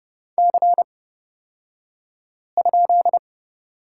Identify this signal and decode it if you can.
Morse code, heard as a keyed tone, about 30 wpm, spaced as for 5 wpm, 700 Hz. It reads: C?
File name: Lesson 36d - BT - Two Characters 30wpm